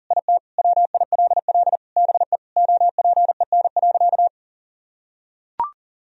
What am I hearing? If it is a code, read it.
Morse code, heard as a keyed tone, about 40 wpm, 700 Hz. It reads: IT WILL BE OPEN.
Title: Sentences from Top 1000 Words - Review Part 1 - 40wpm